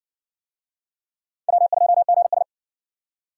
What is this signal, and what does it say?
Morse code, heard as a keyed tone, about 55 wpm, 700 Hz. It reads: W2GS